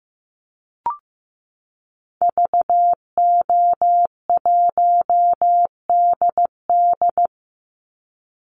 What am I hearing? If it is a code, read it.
Morse code, heard as a keyed tone, about 15 wpm, 700 Hz. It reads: VO1DD
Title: Call Signs Repeated 2x - 15wpm